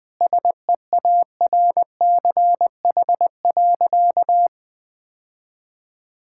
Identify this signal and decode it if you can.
Morse code, heard as a keyed tone, about 20 wpm, 700 Hz. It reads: SEARCH.
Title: Sentences from Top 1000 Words - Review Part 5 - 20wpm